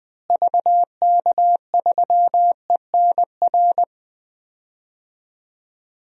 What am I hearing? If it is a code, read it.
Morse code, heard as a keyed tone, about 21 wpm, 700 Hz. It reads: VK3ENR
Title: DXCC-100 Call Signs - 6 Characters - Repeated 3x - 20wpm